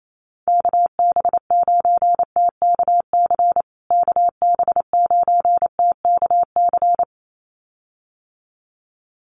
Morse code, heard as a keyed tone, about 28 wpm, 700 Hz. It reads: KB9TKC KB9TKC
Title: USA Call Signs Repeated 2x 28wpm